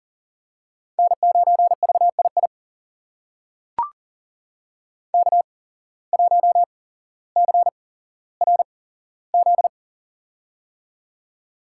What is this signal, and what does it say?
Morse code, heard as a keyed tone, about 40 wpm, spaced as for 17 wpm, 700 Hz. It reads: N9VII K1CRZ
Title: USA Call Signs - 5 Characters - 8x Character Spacing - 40wpm